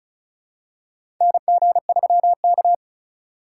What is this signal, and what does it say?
Morse code, heard as a keyed tone, about 36 wpm, 700 Hz. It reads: NG3K
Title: USA Call Signs - 4 Characters - 4x Character Spacing - 35wpm